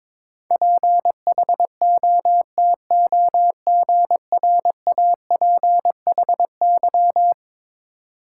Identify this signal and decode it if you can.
Morse code, heard as a keyed tone, about 21 wpm, 700 Hz. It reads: PHOTOGRAPHY
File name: practice / 22wpm / Eleven Letter Words - 22wpm